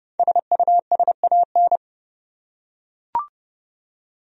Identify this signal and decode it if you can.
Morse code, heard as a keyed tone, about 30 wpm, 700 Hz. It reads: SUSAN
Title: USA Common Names - 30wpm